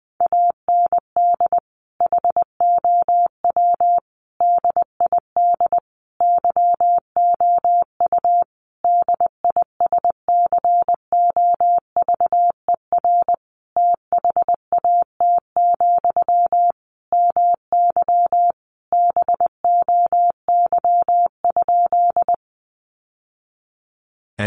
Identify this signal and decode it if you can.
Morse code, heard as a keyed tone, about 20 wpm, 700 Hz. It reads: AND HOW DID YOU DISCOVER THAT, MY BOY?
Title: Sentences from Top 1100 Words 20wpm